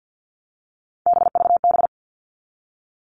Morse code, heard as a keyed tone, about 50 wpm, 700 Hz. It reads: BVB
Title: All Cut Numbers - 3-Digit 50wpm